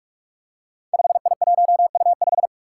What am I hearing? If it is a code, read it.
Morse code, heard as a keyed tone, about 44 wpm, 700 Hz. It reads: 5I1U5
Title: Five Challenging Letter-Number Combinations 45wpm